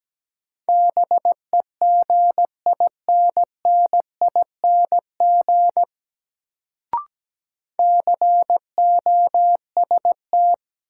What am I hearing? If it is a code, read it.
Morse code, heard as a keyed tone, about 17 wpm, 700 Hz. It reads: BEGINNING COST
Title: Sentences from Top 800 Words - New Word Drill - 17wpm